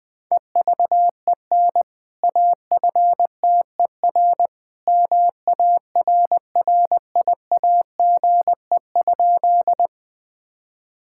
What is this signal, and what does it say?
Morse code, heard as a keyed tone, about 21 wpm, 700 Hz. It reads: EVEN AFTER MARRIAGE?
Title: Sentences from Top 1000 Words - Review Part 5 - 20wpm